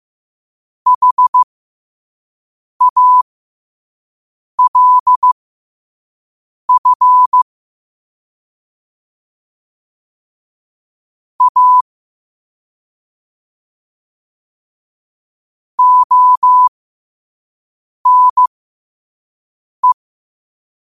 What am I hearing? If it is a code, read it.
Morse code, heard as a keyed tone, about 15 wpm, spaced as for 5 wpm, 1000 Hz. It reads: HALF A ONE
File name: qotdfiles / 5wpm